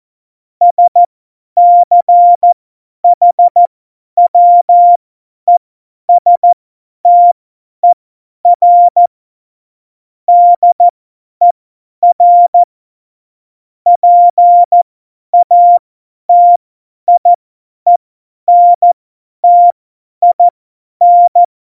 Morse code, heard as a keyed tone, about 14 wpm, spaced as for 10 wpm, 700 Hz. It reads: SCHWESTER DER PATIENTIN
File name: Morsekurs / Witze / Witz 06. Oktober 2020, 50 F70